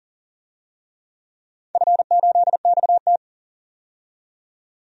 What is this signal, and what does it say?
Morse code, heard as a keyed tone, about 40 wpm, 700 Hz. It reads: F8XT
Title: DXCC-100 Call Signs - 4 Characters - 40wpm